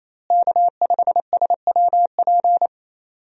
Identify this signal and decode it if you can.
Morse code, heard as a keyed tone, about 27 wpm, 700 Hz. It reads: K5SWP
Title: USA Call Signs - 5 Characters - Repeated 2x - 28wpm